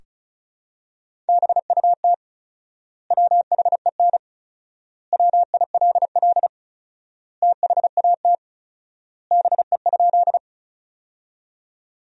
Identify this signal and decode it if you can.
Morse code, heard as a keyed tone, about 36 wpm, 700 Hz. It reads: BUT WHEN WILL THAT BE?
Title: 4x Word-Spacing - Sentences from Top 100 Words - 35wpm